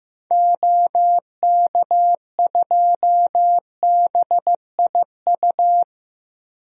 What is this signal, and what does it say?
Morse code, heard as a keyed tone, about 15 wpm, 700 Hz. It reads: OK2BIU
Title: DXCC-100 Call Signs - 6 Characters - Repeated 2x - 15wpm